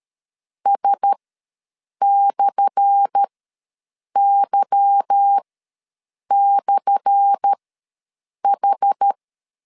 Morse code, heard as a keyed tone, about 13 wpm, spaced as for 7 wpm, 800 Hz. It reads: S/Y/H
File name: LEZIONI / 10 SIGNS / 10A SLASH ws 12wpm 3443